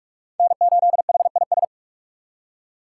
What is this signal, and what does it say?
Morse code, heard as a keyed tone, about 38 wpm, 700 Hz. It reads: N8HIS